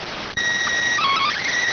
SSTV (slow-scan television), PD 290.